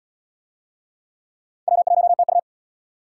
Morse code, heard as a keyed tone, about 68 wpm, 700 Hz. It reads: W1TU